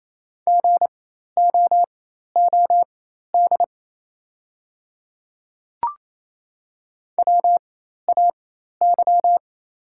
Morse code, heard as a keyed tone, about 28 wpm, spaced as for 13 wpm, 690 Hz. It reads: GOOD WAY
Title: 4x Character Spacing - Top 200 Words - 28wpm